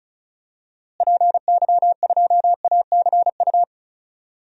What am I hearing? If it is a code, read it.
Morse code, heard as a keyed tone, about 36 wpm, 700 Hz. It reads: PY2ACU
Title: DXCC-100 Call Signs - 6 Characters - Repeated 3x - 35wpm